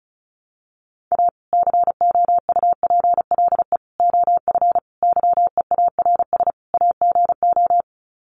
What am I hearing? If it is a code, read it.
Morse code, heard as a keyed tone, about 34 wpm, 700 Hz. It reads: A COUPLE OF YEARS AGO
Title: Sets of 5 Words 35wpm